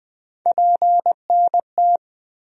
Morse code, heard as a keyed tone, about 20 wpm, 700 Hz. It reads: PNT
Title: USA Call Sign Postfixes - 3 Letters 20wpm